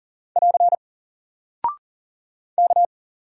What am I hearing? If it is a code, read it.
Morse code, heard as a keyed tone, about 40 wpm, 700 Hz. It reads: <AR> K
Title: Single Character (Letter:Number:Punctuation:Prosign) 40wpm